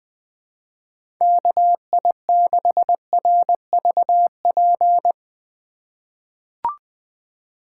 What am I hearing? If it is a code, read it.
Morse code, heard as a keyed tone, about 20 wpm, 700 Hz. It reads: KI6RVP
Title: USA Call Signs - 3x Character Spacing - 20wpm